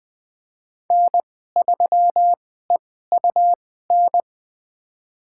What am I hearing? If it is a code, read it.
Morse code, heard as a keyed tone, about 20 wpm, spaced as for 14 wpm, 700 Hz. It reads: N3EUN
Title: USA Call Signs - 2x Character Spacing - 20wpm